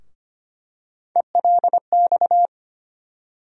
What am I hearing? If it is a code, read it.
Morse code, heard as a keyed tone, about 25 wpm, 700 Hz. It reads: ELX
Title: USA Call Sign Postfixes - 3 Letters 25wpm